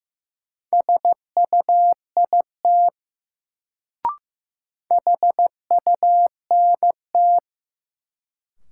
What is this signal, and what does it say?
Morse code, heard as a keyed tone, about 15 wpm, 700 Hz. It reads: SUIT HUNT
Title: Sentences from Top 1600 Words - New Word Drill Alt - 15wpm